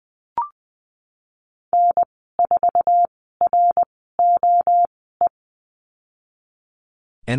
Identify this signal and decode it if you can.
Morse code, heard as a keyed tone, about 20 wpm, spaced as for 15 wpm, 700 Hz. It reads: N4ROE
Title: 173 Lesson 28g - Number 9 - Call Signs 20wpm